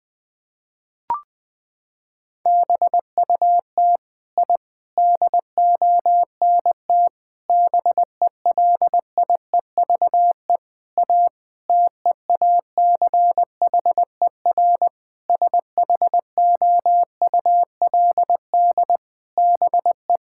Morse code, heard as a keyed tone, about 20 wpm, 700 Hz. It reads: BUT I DONT BELIEVE A TEACHER SHOULD BE
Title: Sentences from Top 1000 Words - Review Part 6 - 20wpm